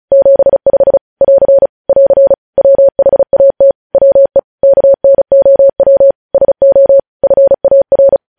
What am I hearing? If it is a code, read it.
Morse code, heard as a keyed tone, about 36 wpm, 550 Hz. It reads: 75 <AR> <AR> WHAT WE KNOW SO FAR
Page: news headlines in morse code at 35 wpm